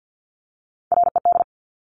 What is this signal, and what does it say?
Morse code, heard as a keyed tone, about 50 wpm, 700 Hz. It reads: RED
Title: Top Three Letter Combinations in Words 50wpm